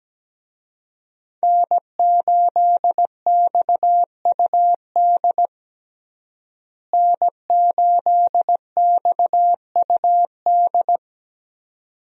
Morse code, heard as a keyed tone, about 17 wpm, 700 Hz. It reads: N8XUD N8XUD
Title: USA Call Signs - 5 Characters - Repeated 2x - 17wpm